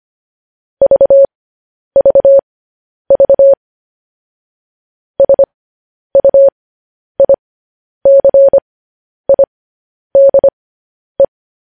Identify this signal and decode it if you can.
Morse code, heard as a keyed tone, about 25 wpm, spaced as for 10 wpm, 550 Hz. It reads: VVV SUICIDE
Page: news headlines in morse code at 10 wpm